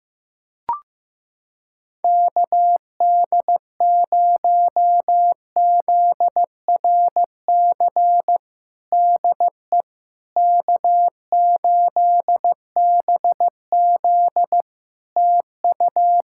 Morse code, heard as a keyed tone, about 15 wpm, 700 Hz. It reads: KD0ZRC DE K8BZ TU
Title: Practice QSO - Exchange 4 - Ending - 15wpm